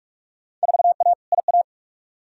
Morse code, heard as a keyed tone, about 41 wpm, 700 Hz. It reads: 4A IA
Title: Field Day Exchanges - 45wpm